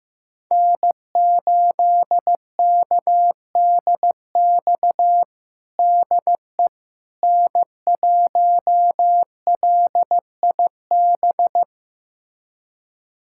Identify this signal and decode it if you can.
Morse code, heard as a keyed tone, about 15 wpm, 700 Hz. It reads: N8KDX DE N1LIB
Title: USA Callsign DE Callsign - 5 Characters - 15wpm